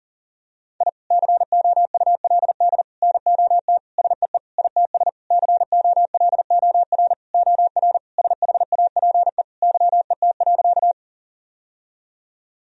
Morse code, heard as a keyed tone, about 40 wpm, 700 Hz. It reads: I COULD NOT SEE ITS COLOR OR SHAPE YET.
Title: Sentences from Top 1000 Words - Review Part 3 - 40wpm